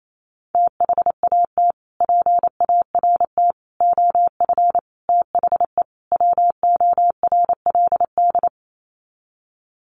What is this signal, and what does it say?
Morse code, heard as a keyed tone, about 28 wpm, 700 Hz. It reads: THAT PART OF THE WORLD
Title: Sets of 5 Words Encore2 28wpm